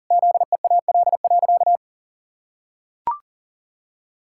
Morse code, heard as a keyed tone, about 40 wpm, 700 Hz. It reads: ZEAL.
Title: Pangram Challenge v2 - 40wpm